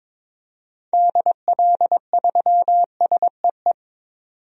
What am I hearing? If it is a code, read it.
Morse code, heard as a keyed tone, about 21 wpm, 700 Hz. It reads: DL3SEE
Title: DXCC-100 Call Signs - 6 Characters - 22wpm